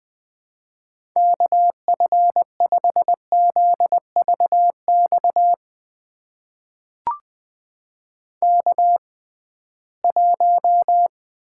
Morse code, heard as a keyed tone, about 20 wpm, spaced as for 15 wpm, 700 Hz. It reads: KF5ZVX K1
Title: USA Call Signs - 6x Character Spacing - 20wpm